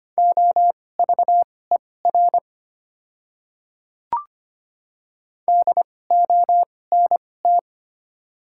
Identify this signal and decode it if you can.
Morse code, heard as a keyed tone, about 25 wpm, spaced as for 18 wpm, 690 Hz. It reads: OVER DONT
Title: 2x Character Spacing - Top 300 Words - 25wpm